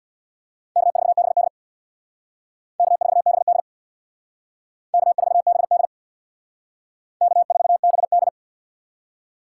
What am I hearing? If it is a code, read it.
Morse code, heard as a keyed tone, about 60 wpm, 700 Hz. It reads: K4BD K4BD K4BD K4BD
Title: Speed Racing - USA 1x2 Call Signs 50wpm